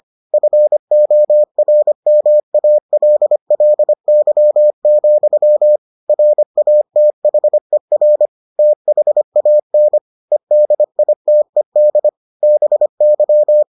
Morse code, heard as a keyed tone, about 25 wpm, 600 Hz. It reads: FORMALLY, RATHER THAN EDITED BY